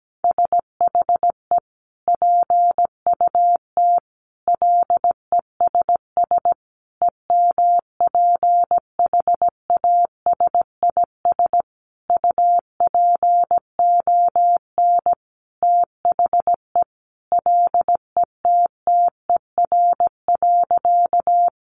Morse code, heard as a keyed tone, about 17 wpm, 700 Hz. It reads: SHE PUT LESS EMPHASIS UPON THE LETTER.
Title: Sentences from Top 1400 Words 17wpm